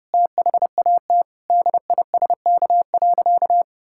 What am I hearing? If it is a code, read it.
Morse code, heard as a keyed tone, about 29 wpm, 700 Hz. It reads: THAT DISK.
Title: Sentences from Top 1000 Words - Review Part 6 - 30wpm